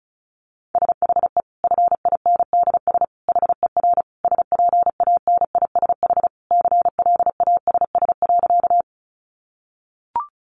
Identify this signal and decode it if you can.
Morse code, heard as a keyed tone, about 34 wpm, 700 Hz. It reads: SHE FINDS HER SPANISH CLASS.
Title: Additional Easy Phrases 35wpm